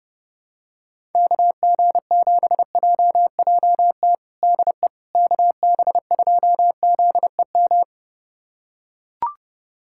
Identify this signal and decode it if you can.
Morse code, heard as a keyed tone, about 30 wpm, 700 Hz. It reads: KG7JJT DE KB2ZEM
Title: USA Callsign DE Callsign - 6 Characters - 30wpm